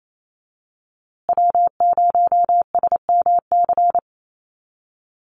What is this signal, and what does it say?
Morse code, heard as a keyed tone, about 28 wpm, 700 Hz. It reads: W0SMC
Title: USA Call Signs Repeated 2x 28wpm